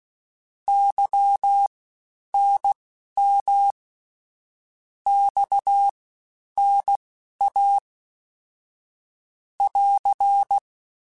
Morse code, heard as a keyed tone, about 16 wpm, spaced as for 10 wpm, 800 Hz. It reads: YNM XNA <AR>
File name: CW Comparisons / CW Comparisons-mp3s / CW Comparisons Broken Characters @15